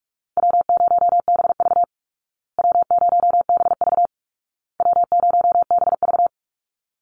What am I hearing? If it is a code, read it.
Morse code, heard as a keyed tone, about 46 wpm, 700 Hz. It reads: W0BV W0BV W0BV
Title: Easy Call Signs Repeated 3x - 45wpm